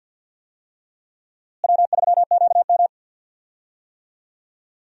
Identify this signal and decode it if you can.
Morse code, heard as a keyed tone, about 51 wpm, 700 Hz. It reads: W3QM
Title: USA Call Signs - 4 Characters - Repeated 3x - 50wpm